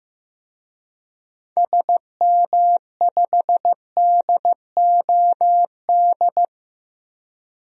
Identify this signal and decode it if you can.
Morse code, heard as a keyed tone, about 15 wpm, 700 Hz. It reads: SM5DOD
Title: DXCC-100 Call Signs - 6 Characters - 15wpm